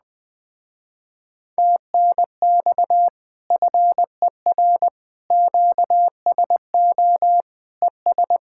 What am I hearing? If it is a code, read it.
Morse code, heard as a keyed tone, about 20 wpm, 700 Hz. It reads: TNX FER QSO ES
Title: Speed-Racing Common QSO Phrases - 17wpm